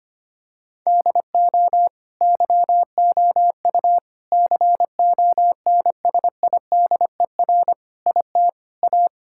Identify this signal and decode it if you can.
Morse code, heard as a keyed tone, about 25 wpm, 700 Hz. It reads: DO YOU CONSIDER IT A
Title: Sentences from Top 1000 Words - Review Part 6 - 25wpm